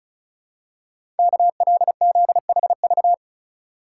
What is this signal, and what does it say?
Morse code, heard as a keyed tone, about 36 wpm, 700 Hz. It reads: KLZHV
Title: Code Groups - 5 Letters - 35wpm